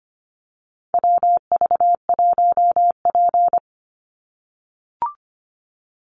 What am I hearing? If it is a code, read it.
Morse code, heard as a keyed tone, about 25 wpm, 700 Hz. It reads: WV1P
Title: USA Call Signs 25wpm